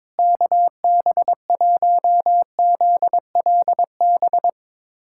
Morse code, heard as a keyed tone, about 22 wpm, 700 Hz. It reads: KB1ZLB
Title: USA Callsign DE Callsign 22wpm